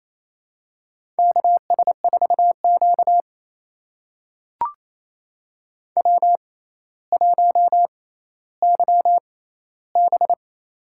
Morse code, heard as a keyed tone, about 28 wpm, spaced as for 14 wpm, 700 Hz. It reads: KS4Q W1YB
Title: USA Call Signs - 4 Characters - 6x Character Spacing - 28wpm